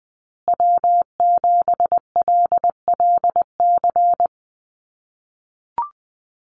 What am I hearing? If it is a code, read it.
Morse code, heard as a keyed tone, about 20 wpm, 700 Hz. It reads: W7LLC
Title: USA Call Signs Repeated 2x 20wpm